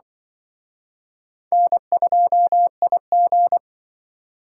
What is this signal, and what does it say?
Morse code, heard as a keyed tone, about 24 wpm, 700 Hz. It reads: N2IG
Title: Speed Racing - USA 1x2 Call Signs 20wpm